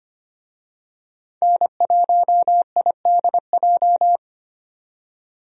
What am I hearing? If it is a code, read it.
Morse code, heard as a keyed tone, about 25 wpm, 700 Hz. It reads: N1IDJ
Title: USA Call Signs - 5 Characters - 25wpm